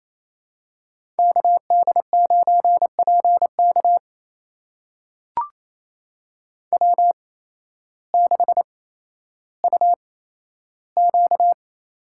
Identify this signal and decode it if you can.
Morse code, heard as a keyed tone, about 27 wpm, spaced as for 13 wpm, 700 Hz. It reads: KD9PK W6UQ
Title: USA Call Signs - 5 Characters - 8x Character Spacing - 28wpm